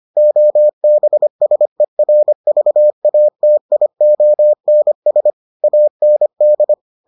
Morse code, heard as a keyed tone, about 25 wpm, 600 Hz. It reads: OBSERVATIONS AND